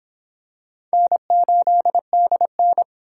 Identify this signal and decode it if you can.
Morse code, heard as a keyed tone, about 25 wpm, 700 Hz. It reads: N8DN